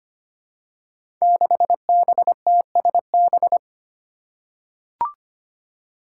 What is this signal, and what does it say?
Morse code, heard as a keyed tone, about 25 wpm, 700 Hz. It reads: 6BTSB
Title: Five Challenging Letter-Number Combinations 25wpm